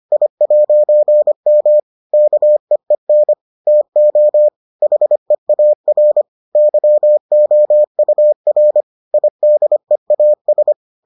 Morse code, heard as a keyed tone, about 25 wpm, 600 Hz. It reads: I'M KEEN TO HEAR YOUR IDEAS